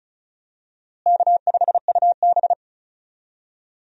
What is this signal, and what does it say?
Morse code, heard as a keyed tone, about 36 wpm, 700 Hz. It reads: K5UB
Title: USA Call Signs - 4 Characters - Repeated 2x - 35wpm